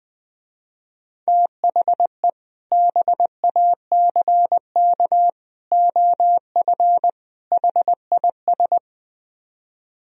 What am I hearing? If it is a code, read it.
Morse code, heard as a keyed tone, about 20 wpm, 700 Hz. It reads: THE BACK OF HIS
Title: Speed-Racing - Sets of 4 Words 17wpm